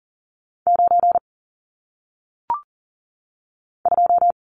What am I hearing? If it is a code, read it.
Morse code, heard as a keyed tone, about 40 wpm, 700 Hz. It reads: 9 2
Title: Numbers - 1-Digit 40wpm